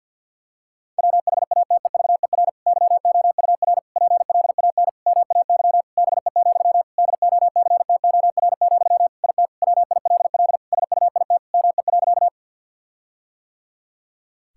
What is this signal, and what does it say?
Morse code, heard as a keyed tone, about 52 wpm, 700 Hz. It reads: WHATEVER YOUR PLAN MAY BE, DOCTOR, IT WILL SUIT ME.
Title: Sentences from Top 1600 Words 50wpm